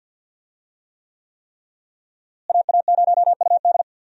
Morse code, heard as a keyed tone, about 53 wpm, 700 Hz. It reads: AA0UD
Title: USA Call Signs - 5 Characters - Repeated 3x - 50wpm